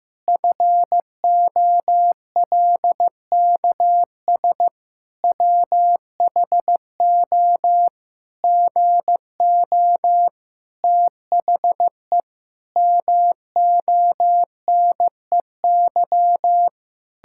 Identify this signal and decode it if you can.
Morse code, heard as a keyed tone, about 15 wpm, 700 Hz. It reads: FOLKS WHO GO THE MONEY
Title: Pangram Challenge v2 - 15wpm